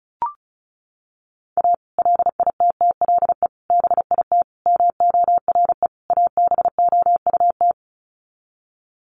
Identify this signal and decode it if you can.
Morse code, heard as a keyed tone, about 34 wpm, 700 Hz. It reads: A LITTLE BIT MORE ABOUT
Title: Sets of 5 Words Encore2 35wpm